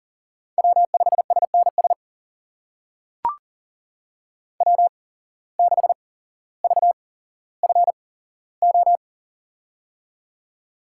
Morse code, heard as a keyed tone, about 39 wpm, spaced as for 17 wpm, 700 Hz. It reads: W5SNS W6VFO
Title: USA Call Signs - 5 Characters - 8x Character Spacing - 40wpm